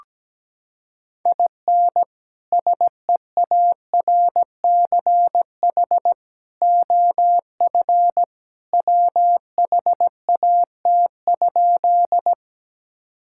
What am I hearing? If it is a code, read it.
Morse code, heard as a keyed tone, about 17 wpm, 700 Hz. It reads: IN SEARCH OF WHAT?
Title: Sentences from Top 1000 Words - Review Part 5 - 17wpm